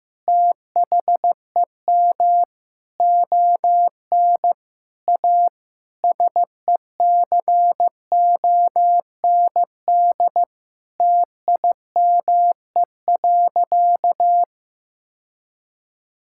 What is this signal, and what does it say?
Morse code, heard as a keyed tone, about 15 wpm, 700 Hz. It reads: THEM ON A SECOND TIME.
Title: Sentences from Top 1000 Words - Review Part 5 - 15wpm